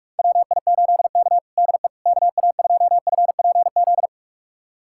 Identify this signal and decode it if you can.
Morse code, heard as a keyed tone, about 46 wpm, 690 Hz. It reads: WI8K DE KA2FPZ
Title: USA Callsign DE Callsign 45wpm